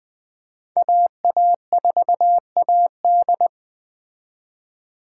Morse code, heard as a keyed tone, about 21 wpm, 700 Hz. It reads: AA4AD
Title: Speed Racing - USA 2x2 Call Signs 17wpm